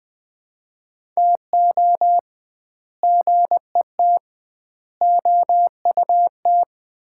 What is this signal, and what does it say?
Morse code, heard as a keyed tone, about 20 wpm, 700 Hz. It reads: TO GET OUT
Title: 2x Word-Spacing - Sets of 4 Words - 20wpm